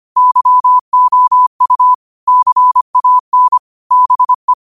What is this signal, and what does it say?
Morse code, heard as a keyed tone, about 25 wpm, 1000 Hz. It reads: YOU CAN BE
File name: qotdfiles / 25wpm